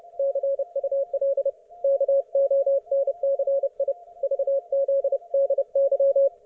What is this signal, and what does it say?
Morse code, heard as a keyed tone, about 30 wpm, 550 Hz, with radio noise. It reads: CUL KONCI VZDY
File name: Archiv / CUC181 / ok1fhi cul181